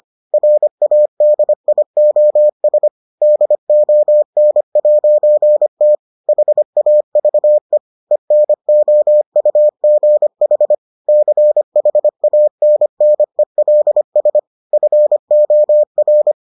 Morse code, heard as a keyed tone, about 25 wpm, 600 Hz. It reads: RADIOS DON'T HAVE ENOUGH CHANNELS FOR